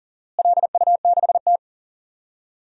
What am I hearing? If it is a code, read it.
Morse code, heard as a keyed tone, about 37 wpm, 700 Hz. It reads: LU6T
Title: DXCC-100 Call Signs - 4 Characters - Repeated 2x - 40wpm